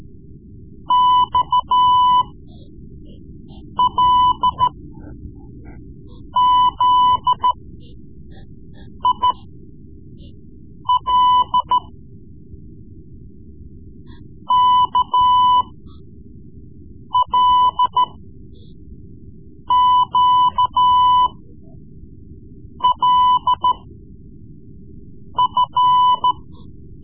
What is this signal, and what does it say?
Morse code, heard as a keyed tone, about 13 wpm, spaced as for 5 wpm, 1000 Hz. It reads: XLZIL KLQLF
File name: Morse Lessons / Morse Lessons / Tape-3B